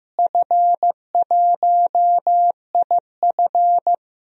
Morse code, heard as a keyed tone, about 16 wpm, 700 Hz. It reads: F1IF
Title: DXCC-100 Call Signs - 4 Characters - Repeated 3x - 15wpm